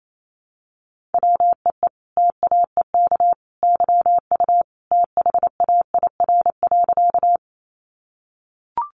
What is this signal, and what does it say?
Morse code, heard as a keyed tone, about 28 wpm, 700 Hz. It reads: WEE TAEK YU THAIR.
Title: A Sound Of Thunder 28wpm